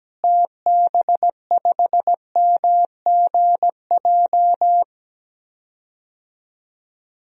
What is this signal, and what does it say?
Morse code, heard as a keyed tone, about 17 wpm, 700 Hz. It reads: TB5MGJ
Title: DXCC-100 Call Signs - 6 Characters - Repeated 3x - 17wpm